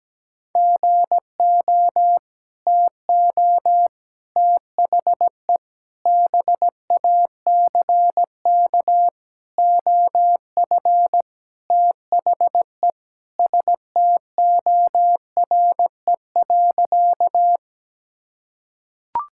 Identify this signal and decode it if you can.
Morse code, heard as a keyed tone, about 17 wpm, 700 Hz. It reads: GO TO THE BACK OF THE STORE.
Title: Sentences from Top 1000 Words - Review Part 5 - 17wpm